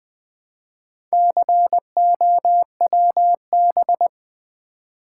Morse code, heard as a keyed tone, about 20 wpm, 700 Hz. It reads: COWB
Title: Build A Word - 2 Letters At A Time - 6-Letter Words - 20wpm